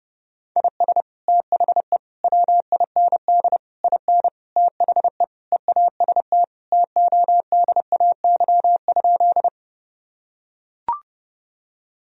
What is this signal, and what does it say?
Morse code, heard as a keyed tone, about 32 wpm, 700 Hz. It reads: IS THE WIND IN THE EAST TODAY?
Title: Sentences from Top 1000 Words - Review Part 3 - 30wpm